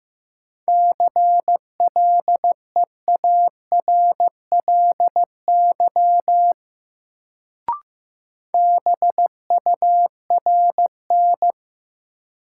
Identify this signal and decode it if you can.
Morse code, heard as a keyed tone, about 16 wpm, 700 Hz. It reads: CLEARLY BURN
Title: Sentences from Top 1300 Words - New Word Drill Alt - 15wpm